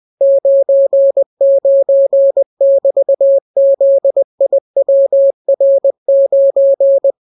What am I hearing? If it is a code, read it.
Morse code, heard as a keyed tone, about 20 wpm, 550 Hz. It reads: 99<BT>ZIWR9